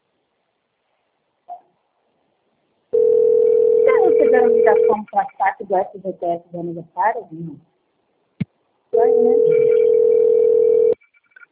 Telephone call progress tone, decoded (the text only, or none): ringback tone